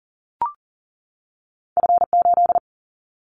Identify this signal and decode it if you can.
Morse code, heard as a keyed tone, about 40 wpm, 700 Hz. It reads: F8
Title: Callsign Prefixes for Top 25 DXCC - 40wpm